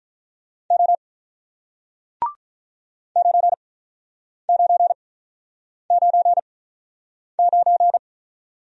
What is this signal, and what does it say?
Morse code, heard as a keyed tone, about 48 wpm, spaced as for 9 wpm, 700 Hz. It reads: O 9999